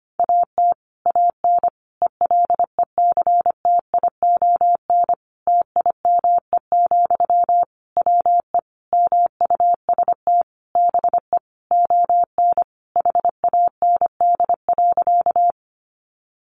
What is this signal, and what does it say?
Morse code, heard as a keyed tone, about 25 wpm, 700 Hz. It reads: AT AN ELECTION TIME, WE MUST BE ON HAND.
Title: Sentences from Top 1500 Words 25wpm